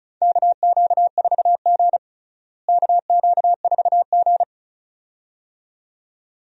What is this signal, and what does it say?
Morse code, heard as a keyed tone, about 36 wpm, 700 Hz. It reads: KQ4G KQ4G
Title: USA Call Signs - 4 Characters - Repeated 2x - 35wpm